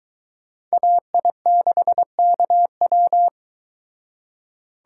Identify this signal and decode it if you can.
Morse code, heard as a keyed tone, about 22 wpm, 700 Hz. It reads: AI6KW